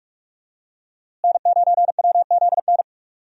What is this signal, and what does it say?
Morse code, heard as a keyed tone, about 46 wpm, 700 Hz. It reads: N9WZN
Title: USA Call Signs - 5 Characters - Repeated 3x - 45wpm